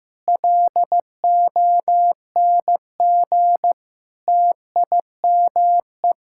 Morse code, heard as a keyed tone, about 15 wpm, 700 Hz. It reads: LONG TIME
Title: Sentences from Top 1000 Words - Review Part 6 - 15wpm